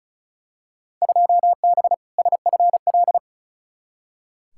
Morse code, heard as a keyed tone, about 34 wpm, 700 Hz. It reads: 2B SFL
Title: Field Day Exchanges - 35wpm